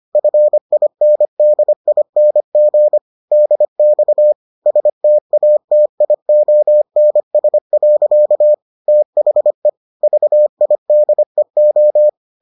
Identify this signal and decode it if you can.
Morse code, heard as a keyed tone, about 25 wpm, 600 Hz. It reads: FINDING DX STATIONS. THE VIDEO